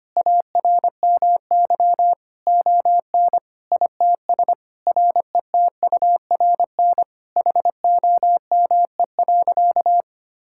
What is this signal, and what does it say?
Morse code, heard as a keyed tone, about 25 wpm, 700 Hz. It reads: ARMY ON ITS RETURN HOME.
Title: Sentences from Top 1000 Words - Review Part 5 - 25wpm